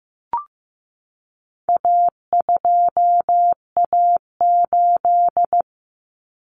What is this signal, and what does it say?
Morse code, heard as a keyed tone, about 15 wpm, 700 Hz. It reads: A2A8
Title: Common Cut Numbers - 4-Digit 15wpm